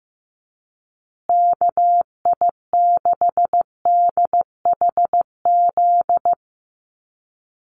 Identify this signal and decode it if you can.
Morse code, heard as a keyed tone, about 15 wpm, 700 Hz. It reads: KI6DHZ
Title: USA Call Signs Repeated 2x 15wpm